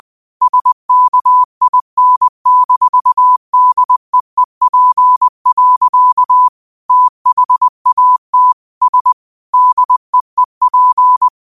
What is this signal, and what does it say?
Morse code, heard as a keyed tone, about 20 wpm, 1000 Hz. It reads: SKIN-DEEP. THAT S DEEP
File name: qotdfiles / 20wpm